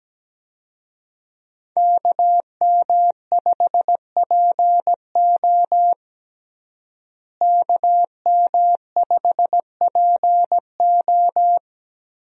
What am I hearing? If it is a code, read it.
Morse code, heard as a keyed tone, about 17 wpm, 700 Hz. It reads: KM5PO KM5PO
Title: USA Call Signs - 5 Characters - Repeated 2x - 17wpm